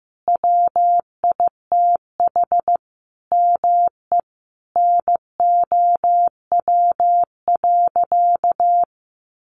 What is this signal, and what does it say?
Morse code, heard as a keyed tone, about 14 wpm, 700 Hz. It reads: WITH ME NOW.
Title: Sentences from Top 1500 Words 15wpm